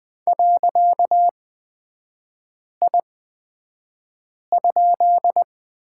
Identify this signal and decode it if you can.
Morse code, heard as a keyed tone, about 20 wpm, spaced as for 5 wpm, 700 Hz. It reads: .I?
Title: 255 Lesson 41e - Period - Three Characters 20wpm